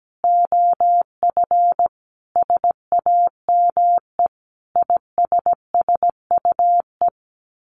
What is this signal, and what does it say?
Morse code, heard as a keyed tone, about 17 wpm, 700 Hz. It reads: OF SAME ISSUE
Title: Sets of 5 Words Encore2 17wpm